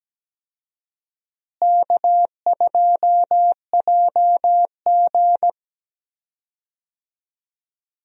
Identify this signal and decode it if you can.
Morse code, heard as a keyed tone, about 17 wpm, 700 Hz. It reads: K2JG
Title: USA Call Signs - 4 Characters - Repeated 2x - 17wpm